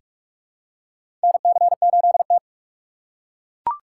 Morse code, heard as a keyed tone, about 42 wpm, 700 Hz. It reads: NC8T
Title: USA Call Signs - 4x Character Spacing - 45wpm